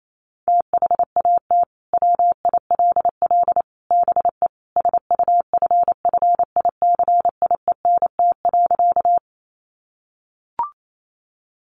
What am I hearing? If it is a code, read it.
Morse code, heard as a keyed tone, about 28 wpm, 700 Hz. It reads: THAT WILL BE SUFFICIENT.
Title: Sentences from Top 1400 Words 28wpm